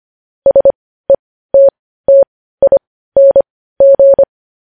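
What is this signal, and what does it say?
Morse code, heard as a keyed tone, about 25 wpm, spaced as for 15 wpm, 550 Hz. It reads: SETTING
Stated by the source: News headlines sent in Morse code at 15 wpm.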